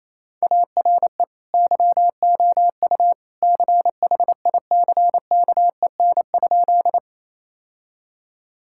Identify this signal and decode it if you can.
Morse code, heard as a keyed tone, about 27 wpm, 700 Hz. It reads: ARE YOU CHICKEN?
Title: Sentences from Top 1000 Words - Review Part 5 - 28wpm